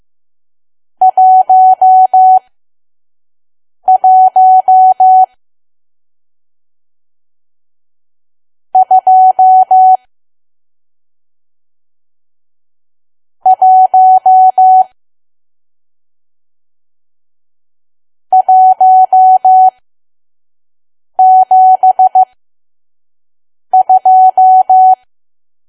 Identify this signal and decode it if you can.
Morse code, heard as a keyed tone, about 15 wpm, spaced as for 5 wpm, 750 Hz. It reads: 11 2 1 172
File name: Archive / 5 WPM files / PM